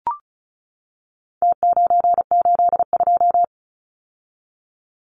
Morse code, heard as a keyed tone, about 34 wpm, 700 Hz. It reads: T982
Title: Numbers - Contest Style - 4-Digit 35wpm